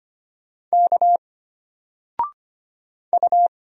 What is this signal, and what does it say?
Morse code, heard as a keyed tone, about 25 wpm, 700 Hz. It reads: K U
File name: Single-Letter-Number-25wpm